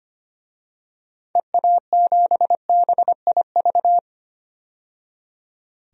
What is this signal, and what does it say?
Morse code, heard as a keyed tone, about 25 wpm, 700 Hz. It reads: EA7BIV
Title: DXCC-100 Call Signs - 6 Characters - 25wpm